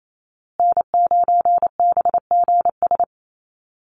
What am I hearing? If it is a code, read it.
Morse code, heard as a keyed tone, about 28 wpm, 700 Hz. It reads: N9BGS
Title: USA Call Signs Repeated 3x 28wpm